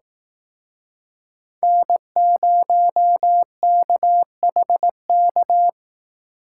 Morse code, heard as a keyed tone, about 18 wpm, 700 Hz. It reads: N0KHK
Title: Speed Racing - USA Call Signs 15wpm